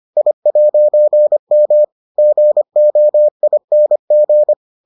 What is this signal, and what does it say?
Morse code, heard as a keyed tone, about 25 wpm, 600 Hz. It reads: I'M GOING